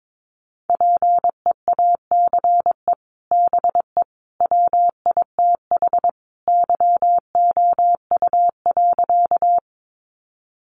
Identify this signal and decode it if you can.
Morse code, heard as a keyed tone, about 22 wpm, 700 Hz. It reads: PEACE BE WITH YOU.
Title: Sentences from Top 1500 Words 22wpm